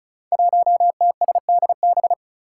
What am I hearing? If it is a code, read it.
Morse code, heard as a keyed tone, about 36 wpm, 700 Hz. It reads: 1TSDB
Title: Five Challenging Letter-Number Combinations 35wpm